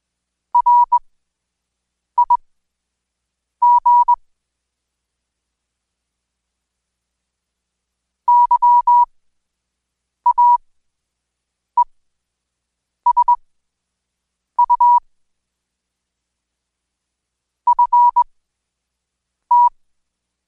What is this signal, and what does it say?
Morse code, heard as a keyed tone, about 21 wpm, spaced as for 6 wpm, 970 Hz. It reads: RIG YAESU FT
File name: Code Quick/AudioDownld2/Seg109-Day14-DayDrill 14